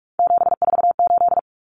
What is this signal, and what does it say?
Morse code, heard as a keyed tone, about 44 wpm, 700 Hz. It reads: 748
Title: Common Cut Numbers - 3-Digit 45wpm